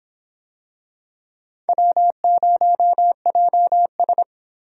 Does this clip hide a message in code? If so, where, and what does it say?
Yes: Morse code from 1.7 to 4.2 s, heard as a keyed tone, about 23 wpm, 700 Hz. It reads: W0JS